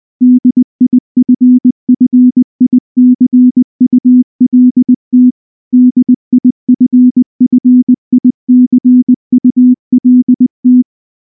Morse code, heard as a keyed tone, about 19 wpm, 250 Hz. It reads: DIFFICULT DIFFICULT
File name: morse-training-20WPM